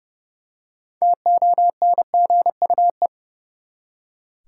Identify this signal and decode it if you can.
Morse code, heard as a keyed tone, about 30 wpm, 700 Hz. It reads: TONGUE
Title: Six Letter Words - 30wpm